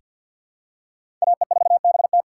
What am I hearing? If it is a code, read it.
Morse code, heard as a keyed tone, about 46 wpm, 700 Hz. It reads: AE4BT
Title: USA Call Signs - 5 Characters - 4x Character Spacing - 50wpm